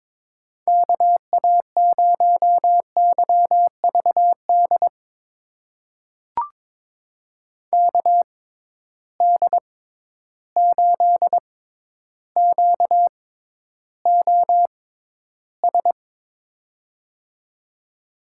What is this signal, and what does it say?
Morse code, heard as a keyed tone, about 22 wpm, spaced as for 11 wpm, 700 Hz. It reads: KA0YVD KD8QOS
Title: USA Call Signs - 6 Characters - 6x Character Spacing - 22wpm